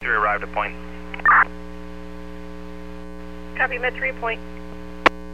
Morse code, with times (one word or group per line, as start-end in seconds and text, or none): none